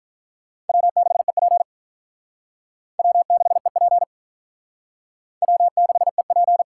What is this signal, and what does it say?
Morse code, heard as a keyed tone, about 48 wpm, 700 Hz. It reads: W6EP W6EP W6EP